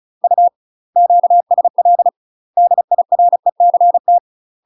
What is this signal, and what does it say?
Morse code, heard as a keyed tone, about 35 wpm, 700 Hz. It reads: U QSL DIRECT